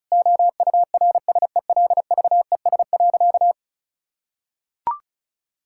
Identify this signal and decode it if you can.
Morse code, heard as a keyed tone, about 36 wpm, 700 Hz. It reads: OURSELVES.
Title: Sentences from Top 1600 Words 35wpm